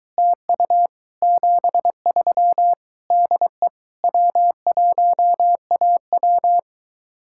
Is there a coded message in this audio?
Yes: Morse code from 0.2 to 6.6 s, heard as a keyed tone, about 23 wpm, 700 Hz. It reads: TU 73 DE W1AW